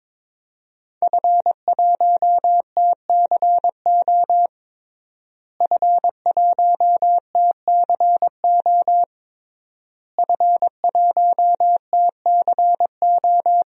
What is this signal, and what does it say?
Morse code, heard as a keyed tone, about 22 wpm, 700 Hz. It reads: F1TCO F1TCO F1TCO
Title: DXCC-100 Call Signs - 5 Characters - Repeated 3x - 22wpm